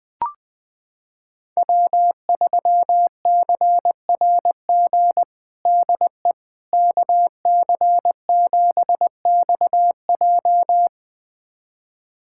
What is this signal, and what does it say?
Morse code, heard as a keyed tone, about 21 wpm, 700 Hz. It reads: W3CRG DE KC7XJ
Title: USA Callsign DE Callsign - 5 Characters - 20wpm